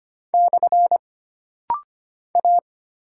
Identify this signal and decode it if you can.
Morse code, heard as a keyed tone, about 25 wpm, 700 Hz. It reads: / A
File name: Single Character (Letter:Number:Punctuation:Prosign) - Rapid-Fire - 25wpm